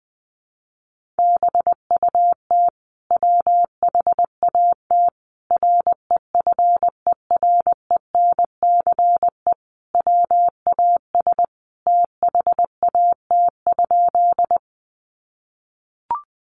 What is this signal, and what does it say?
Morse code, heard as a keyed tone, about 20 wpm, 700 Hz. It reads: BUT WHAT REFERENCE WAS THAT?
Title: Sentences from Top 1200 Words 20wpm